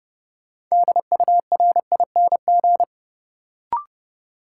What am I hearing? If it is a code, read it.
Morse code, heard as a keyed tone, about 30 wpm, 700 Hz. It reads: DURING
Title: Sets of 2 Words - New Word Drill - 30wpm